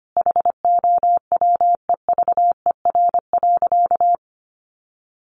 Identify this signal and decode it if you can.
Morse code, heard as a keyed tone, about 25 wpm, 700 Hz. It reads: HOWEVER.
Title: Sentences from Top 600 Words 25wpm